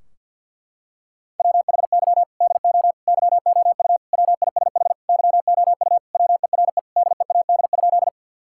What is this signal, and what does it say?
Morse code, heard as a keyed tone, about 48 wpm, 700 Hz. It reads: WHY DO YOU WISH YOU WERE DEAD?
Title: Sentences from Top 1600 Words 50wpm